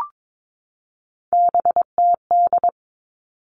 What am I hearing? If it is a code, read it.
Morse code, heard as a keyed tone, about 22 wpm, 700 Hz. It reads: BTD
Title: All Cut Numbers - 3-Digit 22wpm